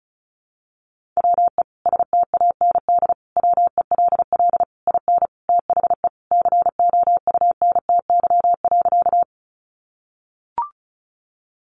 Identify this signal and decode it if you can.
Morse code, heard as a keyed tone, about 34 wpm, 700 Hz. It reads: WE STAND WELL IN THE COUNTY.
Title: Sentences from Top 1200 Words 35wpm